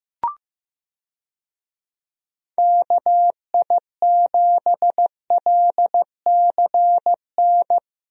Morse code, heard as a keyed tone, about 15 wpm, 700 Hz. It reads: KI7LCN
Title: USA Call Signs - 6 Characters - Repeated 2x - 15wpm